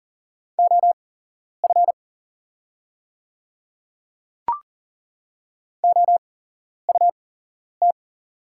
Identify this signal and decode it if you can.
Morse code, heard as a keyed tone, about 40 wpm, spaced as for 11 wpm, 700 Hz. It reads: OF OUT
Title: 8x Character Spacing - Top 100 Words - 40wpm